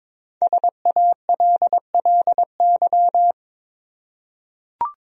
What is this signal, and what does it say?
Morse code, heard as a keyed tone, about 22 wpm, 700 Hz. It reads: SALLY
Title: USA Common Names - 22wpm